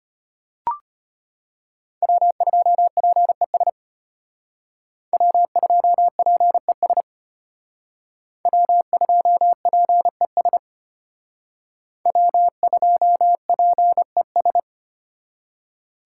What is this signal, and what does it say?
Morse code, heard as a keyed tone, about 32 wpm, 700 Hz. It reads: W2PES W2PES W2PES W2PES